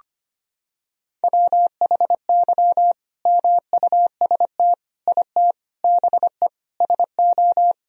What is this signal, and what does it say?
Morse code, heard as a keyed tone, about 25 wpm, 700 Hz. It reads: WHY MUST IT BE SO
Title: Sentences from Top 1000 Words - Review Part 2 - 25wpm